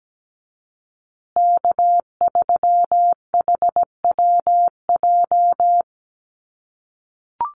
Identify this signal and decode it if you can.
Morse code, heard as a keyed tone, about 17 wpm, 700 Hz. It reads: K3HWJ
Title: USA Call Signs Repeated 3x 17wpm